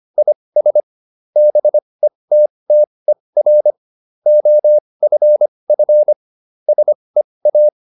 Morse code, heard as a keyed tone, about 25 wpm, spaced as for 20 wpm, 600 Hz. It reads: IS BETTER OFF SEA